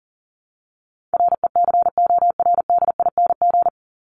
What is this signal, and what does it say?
Morse code, heard as a keyed tone, about 40 wpm, 700 Hz. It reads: RECORDING
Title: Top 1501-1750 Words 40wpm